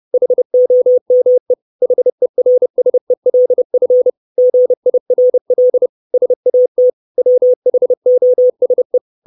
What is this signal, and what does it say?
Morse code, heard as a keyed tone, about 30 wpm, 500 Hz. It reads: HOME HERSELF GIRL SAT WHOSE
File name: 632 Words 30WPM